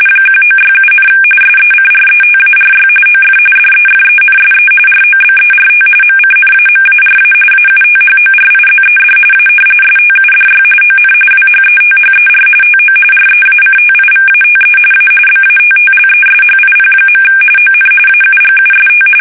RTTY, about 45.45 baud. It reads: 1967, SECRETARY OF DEFENSE ROBERT MCNAMARA AUTHORIZED A STUDY ON WHETHER TO REACTIVATE A BATTLESHIP. ON AUGUST
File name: fsk850 / battleshipnj